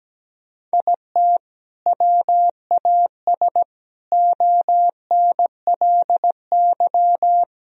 Morse code, heard as a keyed tone, about 17 wpm, 700 Hz. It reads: IT WAS ONLY
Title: Sentences from Top 1000 Words - Review Part 3 - 17wpm